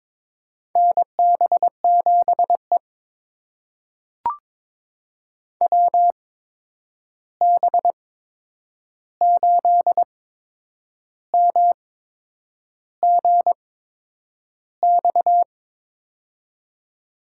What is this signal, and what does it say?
Morse code, heard as a keyed tone, about 22 wpm, spaced as for 8 wpm, 700 Hz. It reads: NB7E WB8MGX
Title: USA Call Signs - 8x Character Spacing - 22wpm